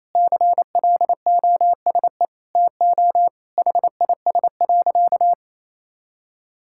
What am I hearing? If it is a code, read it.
Morse code, heard as a keyed tone, about 28 wpm, 700 Hz. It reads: CLOSE TO HIS.
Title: Sentences from Top 1000 Words - Review Part 6 - 28wpm